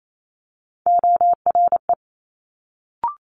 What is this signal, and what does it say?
Morse code, heard as a keyed tone, about 28 wpm, 700 Hz. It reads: ORE
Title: Multi-letter Phonemes Quick Fire 28wpm